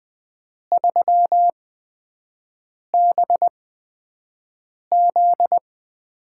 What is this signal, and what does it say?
Morse code, heard as a keyed tone, about 20 wpm, spaced as for 5 wpm, 700 Hz. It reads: 3BZ
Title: USA Call Signs - 6 Characters - 8x Character Spacing - 20wpm